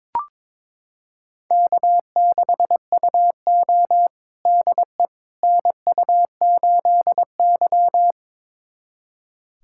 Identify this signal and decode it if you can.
Morse code, heard as a keyed tone, about 22 wpm, 700 Hz. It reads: K6UO DE NU8Y
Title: USA Callsign DE Callsign - 4 Characters - 22wpm